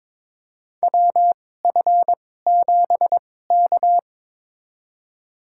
Morse code, heard as a keyed tone, about 22 wpm, spaced as for 16 wpm, 700 Hz. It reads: WF7K
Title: USA Call Signs - 4 Characters - 2x Character Spacing - 22wpm